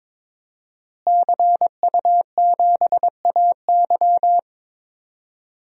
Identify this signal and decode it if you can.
Morse code, heard as a keyed tone, about 22 wpm, 700 Hz. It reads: CU7AY
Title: DXCC-100 Call Signs - 5 Characters - Repeated 3x - 22wpm